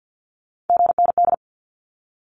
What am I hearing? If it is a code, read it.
Morse code, heard as a keyed tone, about 50 wpm, 700 Hz. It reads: GND
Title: Abbreviations Repeated 3x - 50wpm